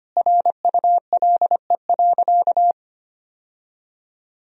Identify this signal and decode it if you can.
Morse code, heard as a keyed tone, about 25 wpm, 700 Hz. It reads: RULE.
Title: Sentences from Top 1000 Words - Review Part 2 - 25wpm